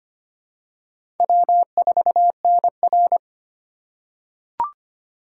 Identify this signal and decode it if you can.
Morse code, heard as a keyed tone, about 25 wpm, 700 Hz. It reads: W4NR
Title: USA Call Signs - 4 Characters - 2x Character Spacing - 25wpm